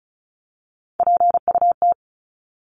Morse code, heard as a keyed tone, about 34 wpm, 700 Hz. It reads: PUT
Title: Top 300 Words 35wpm